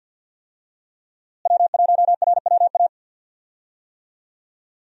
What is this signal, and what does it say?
Morse code, heard as a keyed tone, about 52 wpm, 700 Hz. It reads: W1RWA
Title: USA Call Signs - 5 Characters - Repeated 2x - 50wpm